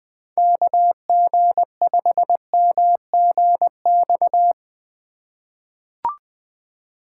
Morse code, heard as a keyed tone, about 20 wpm, 700 Hz. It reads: KG5MGX
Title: USA Call Signs - 6 Characters - 3x Character Spacing - 20wpm